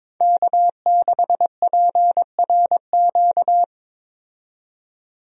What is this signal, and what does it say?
Morse code, heard as a keyed tone, about 23 wpm, 700 Hz. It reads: K6PRQ
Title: USA Call Signs - 5 Characters - 22wpm